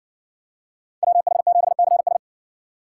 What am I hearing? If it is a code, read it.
Morse code, heard as a keyed tone, about 60 wpm, 700 Hz. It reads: WH7CS